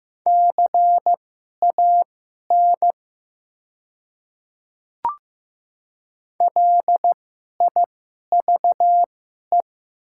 Morse code, heard as a keyed tone, about 15 wpm, spaced as for 11 wpm, 700 Hz. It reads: CAN LIVE
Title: 2x Character Spacing - Top 200 Words - 15wpm